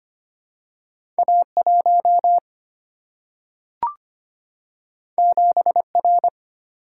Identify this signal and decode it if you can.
Morse code, heard as a keyed tone, about 25 wpm, 700 Hz. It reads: A1 7R
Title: Single-Letter Single-Number 25wpm